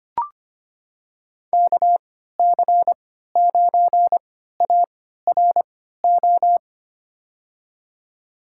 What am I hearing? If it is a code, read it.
Morse code, heard as a keyed tone, about 25 wpm, spaced as for 14 wpm, 700 Hz. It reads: KC9ARO
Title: USA Call Signs - 3x Character Spacing - 25wpm